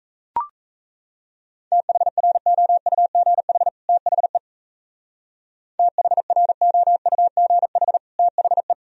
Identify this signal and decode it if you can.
Morse code, heard as a keyed tone, about 35 wpm, 700 Hz. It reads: THROUGH THE THROUGH THE
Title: Speed-Racing - Sets of 2 Words 28wpm